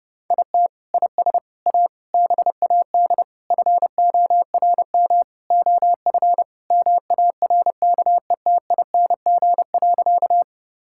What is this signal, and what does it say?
Morse code, heard as a keyed tone, about 30 wpm, 700 Hz. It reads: IT IS A BAD FORM OF MARKETING.
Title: Sentences from Top 1000 Words - Review Part 5 - 30wpm